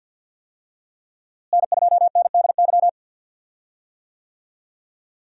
Morse code, heard as a keyed tone, about 52 wpm, 700 Hz. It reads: N2NDY
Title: USA Call Signs - 5 Characters - Repeated 2x - 50wpm